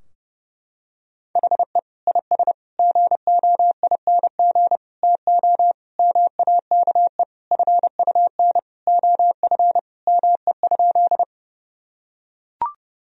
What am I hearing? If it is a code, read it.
Morse code, heard as a keyed tone, about 30 wpm, 700 Hz. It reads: HE IS GOING TO MAKE FUN OF ME?
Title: Sentences from Top 1000 Words - Review Part 4 - 30wpm